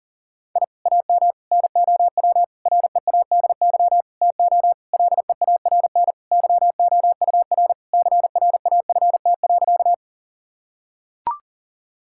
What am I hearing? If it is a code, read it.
Morse code, heard as a keyed tone, about 40 wpm, 700 Hz. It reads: I AM NOW READY TO LEARN YOUR CRAFT.
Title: Sentences from Top 1000 Words - Review Part 6 - 40wpm